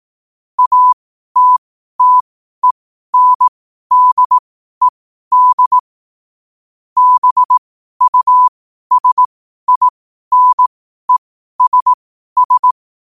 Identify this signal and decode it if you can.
Morse code, heard as a keyed tone, about 18 wpm, spaced as for 12 wpm, 1000 Hz. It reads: ATTENDED BUSINESS
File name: qotdfiles / 12wpm